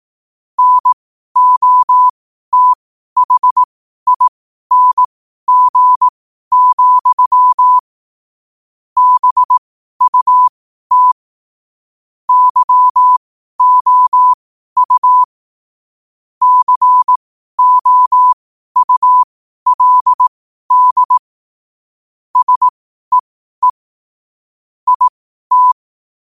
Morse code, heard as a keyed tone, about 18 wpm, spaced as for 12 wpm, 1000 Hz. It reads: NOTHING, BUT YOU COULD SEE IT
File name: qotdfiles / 12wpm